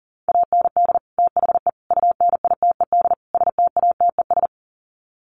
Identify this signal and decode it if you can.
Morse code, heard as a keyed tone, about 40 wpm, 700 Hz. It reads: AND THE UNITED STATES
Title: Sets of 4 Words 40wpm